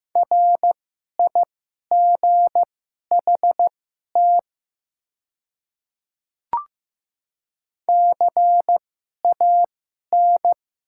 Morse code, heard as a keyed tone, about 15 wpm, spaced as for 11 wpm, 700 Hz. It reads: RIGHT CAN
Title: 2x Character Spacing - Top 200 Words - 15wpm